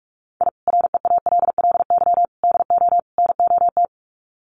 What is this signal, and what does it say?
Morse code, heard as a keyed tone, about 44 wpm, 700 Hz. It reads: I REALLY DO NOT
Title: Sets of 4 Words 45wpm